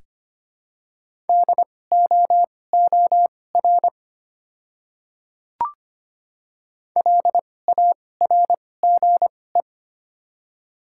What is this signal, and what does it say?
Morse code, heard as a keyed tone, about 25 wpm, spaced as for 18 wpm, 700 Hz. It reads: DOOR LARGE
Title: 2x Character Spacing - Top 300 Words - 25wpm